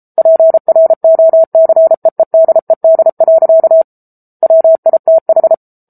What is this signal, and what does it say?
Morse code, heard as a keyed tone, about 34 wpm, 650 Hz. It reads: PROCEEDED. WITH
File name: Book Two / 07-Chapter 7 - 33 WPM